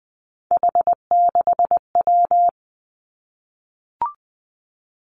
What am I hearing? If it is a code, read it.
Morse code, heard as a keyed tone, about 20 wpm, 700 Hz. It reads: H6W
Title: Three Challenging Letter-Number Combinations 20wpm